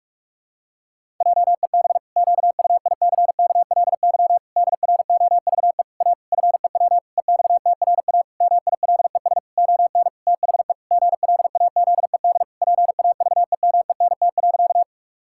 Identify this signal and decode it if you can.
Morse code, heard as a keyed tone, about 46 wpm, 700 Hz. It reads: JEB QUICKLY DROVE A FEW EXTRA MILES ON THE GLAZED PAVEMENT.
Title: Pangram Challenge v2 - 45wpm